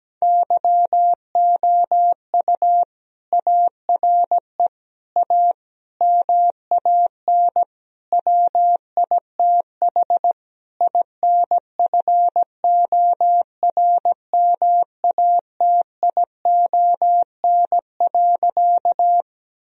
Morse code, heard as a keyed tone, about 17 wpm, 700 Hz. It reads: YOU ARE A MAN WITH INFORMATION.
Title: Sentences from Top 1000 Words - Review Part 3 - 17wpm